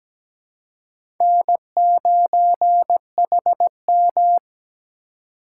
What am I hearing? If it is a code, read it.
Morse code, heard as a keyed tone, about 17 wpm, 700 Hz. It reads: N9HM
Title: USA Call Signs - 4 Characters - 4x Character Spacing - 17wpm